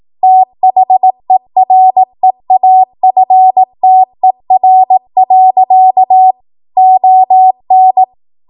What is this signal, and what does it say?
Morse code, heard as a keyed tone, about 18 wpm, 750 Hz. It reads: THEREAFTER. ON
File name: Morse / 18WPM